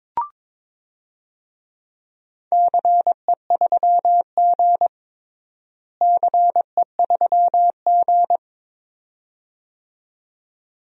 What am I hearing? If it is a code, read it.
Morse code, heard as a keyed tone, about 22 wpm, 700 Hz. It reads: CE3G CE3G
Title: DXCC-100 Call Signs - 4 Characters - Repeated 2x - 22wpm